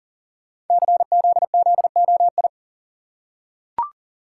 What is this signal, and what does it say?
Morse code, heard as a keyed tone, about 40 wpm, 690 Hz. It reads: CZZOI
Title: Code Groups - 5 Letters - 40wpm